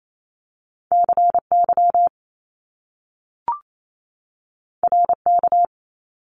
Morse code, heard as a keyed tone, about 28 wpm, 700 Hz. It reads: C Y R K
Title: Top Two Letter Combinations in Words 28wpm